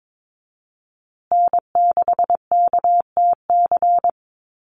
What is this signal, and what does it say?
Morse code, heard as a keyed tone, about 22 wpm, 700 Hz. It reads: N6KTC
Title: USA Call Signs Repeated 3x 22wpm